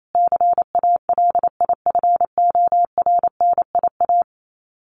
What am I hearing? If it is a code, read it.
Morse code, heard as a keyed tone, about 28 wpm, 700 Hz. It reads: CALIFORNIA
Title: Sets of 5 Words Encore2 28wpm